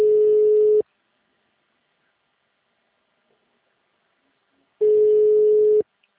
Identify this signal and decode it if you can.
Ringback tone (Europe, 425 Hz).